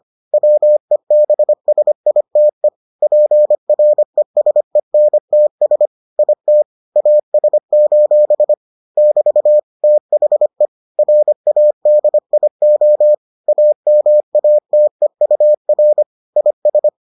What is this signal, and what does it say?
Morse code, heard as a keyed tone, about 25 wpm, 600 Hz. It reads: WEBSITE PRESENTS IT AS: <BT> THE RADIO AMATEUR IS